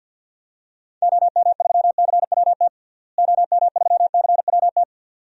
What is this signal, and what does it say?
Morse code, heard as a keyed tone, about 52 wpm, 700 Hz. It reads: OM3CWT OM3CWT
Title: DXCC-100 Call Signs - 6 Characters - Repeated 2x - 50wpm